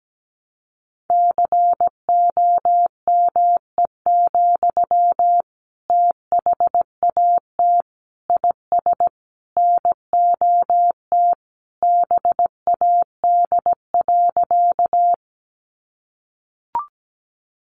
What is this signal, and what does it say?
Morse code, heard as a keyed tone, about 17 wpm, 700 Hz. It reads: COME, THAT IS NOT BAD.
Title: Sentences from Top 700 Words 17wpm